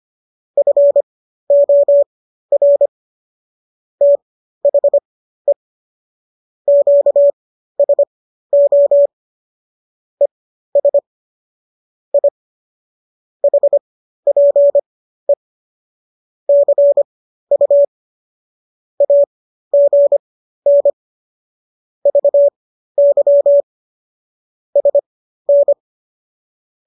Morse code, heard as a keyed tone, about 25 wpm, spaced as for 13 wpm, 570 Hz. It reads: FOR THE QSO ES I HPE CU AGN VY SN